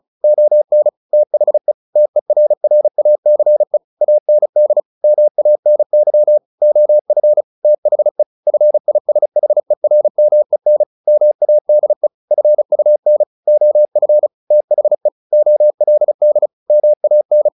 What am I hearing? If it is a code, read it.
Morse code, heard as a keyed tone, about 34 wpm, 600 Hz. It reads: ON THE TERRACE AND MANY OF THE FISHERMEN MADE FUN OF THE OLD MAN
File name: the old man and the sea / the old man and the sea 1 35wpm